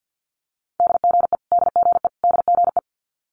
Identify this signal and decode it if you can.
Morse code, heard as a keyed tone, about 50 wpm, 700 Hz. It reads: DGE DGE DGE
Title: Multi-letter Phonemes Repeated 3x 50wpm